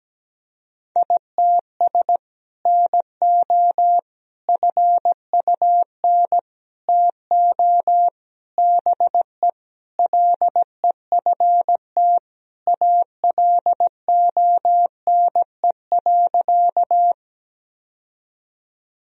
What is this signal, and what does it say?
Morse code, heard as a keyed tone, about 17 wpm, 700 Hz. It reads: ITS NO FUN TO BE LEFT ALONE.
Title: Sentences from Top 1000 Words - Review Part 4 - 17wpm